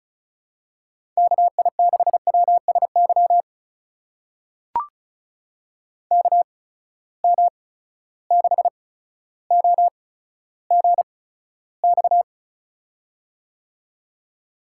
Morse code, heard as a keyed tone, about 34 wpm, spaced as for 15 wpm, 700 Hz. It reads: KI6WSY KM6OGX
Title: USA Call Signs - 6 Characters - 8x Character Spacing - 35wpm